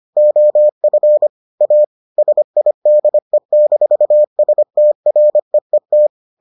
Morse code, heard as a keyed tone, about 25 wpm, 600 Hz. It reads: OF A SIDE-STREET